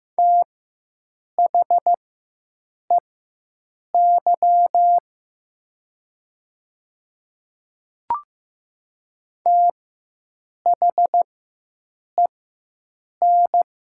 Morse code, heard as a keyed tone, about 15 wpm, spaced as for 7 wpm, 700 Hz. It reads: THEY THEN
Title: 4x Character Spacing - Top 100 Words - 15wpm